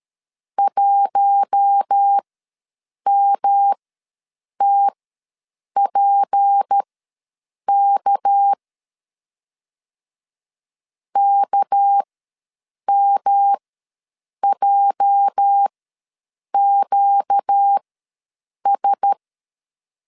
Morse code, heard as a keyed tone, about 13 wpm, spaced as for 7 wpm, 800 Hz. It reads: 1MTPK KMJQS
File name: LEZIONI / 8 YJ45 / 8 YJ ws 12wpm 3342